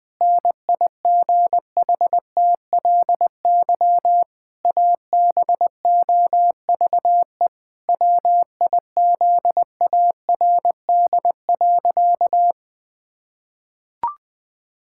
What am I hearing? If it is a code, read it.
Morse code, heard as a keyed tone, about 20 wpm, 700 Hz. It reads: NIGHTLY ABOVE WIZARD.
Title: Pangram Challenge v2 - 20wpm